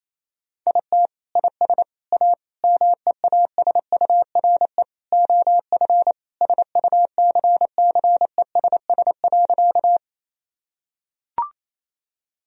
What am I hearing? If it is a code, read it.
Morse code, heard as a keyed tone, about 28 wpm, 700 Hz. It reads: IT IS A MEASURE OF SUCCESS.
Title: Sentences from Top 1000 Words - Review Part 2 - 28wpm